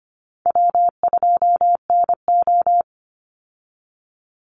Morse code, heard as a keyed tone, about 25 wpm, 700 Hz. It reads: W2NO
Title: Call Signs - 25wpm